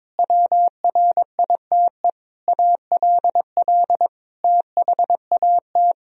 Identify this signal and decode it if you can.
Morse code, heard as a keyed tone, about 22 wpm, 700 Hz. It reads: WRITE ALL THAT
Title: Sentences from Top 1000 Words - Review Part 1 - 22wpm